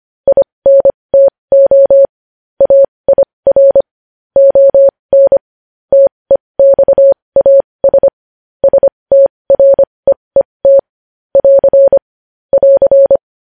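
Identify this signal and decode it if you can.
Morse code, heard as a keyed tone, about 25 wpm, spaced as for 20 wpm, 550 Hz. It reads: INTO AIR ON TEXAS STREET <AR> <AR>
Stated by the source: Morse code news headlines sent at 20.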